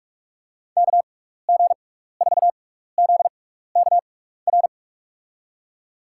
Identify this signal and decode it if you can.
Morse code, heard as a keyed tone, about 47 wpm, spaced as for 16 wpm, 700 Hz. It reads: KG4ZKR
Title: USA Call Signs - 6x Character Spacing - 45wpm